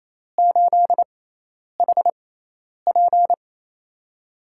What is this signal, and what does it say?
Morse code, heard as a keyed tone, about 28 wpm, spaced as for 10 wpm, 700 Hz. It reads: 8HP
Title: USA Call Signs - 5 Characters - 6x Character Spacing - 28wpm